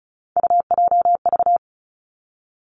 Morse code, heard as a keyed tone, about 36 wpm, 700 Hz. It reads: UJV
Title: Three Challenging Letter-Number Combinations 35wpm